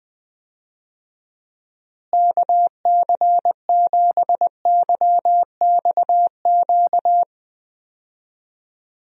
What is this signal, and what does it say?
Morse code, heard as a keyed tone, about 20 wpm, 700 Hz. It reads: KC7YXQ
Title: USA Call Signs - 6 Characters - Repeated 3x - 20wpm